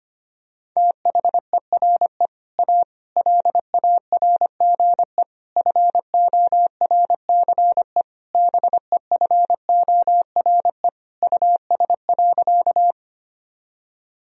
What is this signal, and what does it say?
Morse code, heard as a keyed tone, about 25 wpm, 700 Hz. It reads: THERE A LARGE FORCE BEFORE US.
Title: Sentences from Top 1000 Words - Review Part 3 - 25wpm